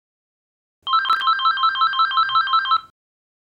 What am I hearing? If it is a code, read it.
There is no code.